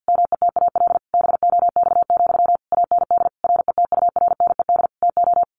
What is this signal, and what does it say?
Morse code, heard as a keyed tone, about 48 wpm, 700 Hz. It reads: METAL BOX, AND RETURNED TO
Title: A Sound Of Thunder 50wpm